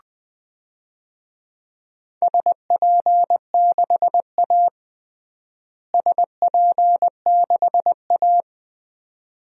Morse code, heard as a keyed tone, about 20 wpm, 700 Hz. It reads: SP6A SP6A
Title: DXCC-100 Call Signs - 4 Characters - Repeated 2x - 20wpm